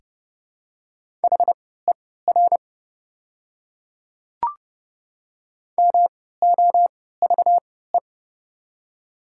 Morse code, heard as a keyed tone, about 32 wpm, spaced as for 17 wpm, 700 Hz. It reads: HER MOVE
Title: 3x Character Spacing - Top 200 Words - 30wpm